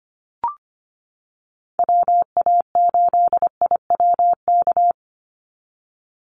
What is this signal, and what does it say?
Morse code, heard as a keyed tone, about 25 wpm, 700 Hz. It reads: WA8IWK
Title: Call Signs - 25wpm